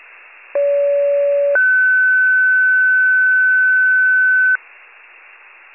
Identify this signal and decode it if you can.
Two-tone page, heard as a one-second tone, then a three-second tone, about 570 then 1500 Hz. It is heard from 0.5 to 4.6 s.